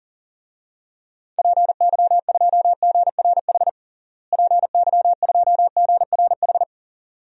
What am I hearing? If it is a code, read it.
Morse code, heard as a keyed tone, about 37 wpm, 700 Hz. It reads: PY2GRH PY2GRH
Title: DXCC-100 Call Signs - 6 Characters - Repeated 2x - 40wpm